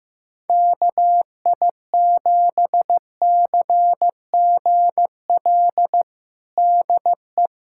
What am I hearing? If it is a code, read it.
Morse code, heard as a keyed tone, about 15 wpm, 700 Hz. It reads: KI7CGL DE
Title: USA Callsign DE Callsign - 6 Characters - 15wpm